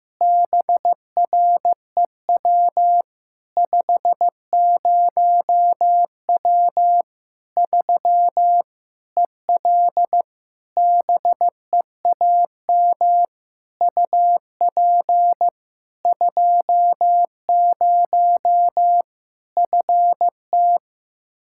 Morse code, heard as a keyed tone, about 15 wpm, 700 Hz. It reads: BREW 50W 3 EL BEAM UP 20 FT
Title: Practice QSO - Exchange 2 - Info - 15wpm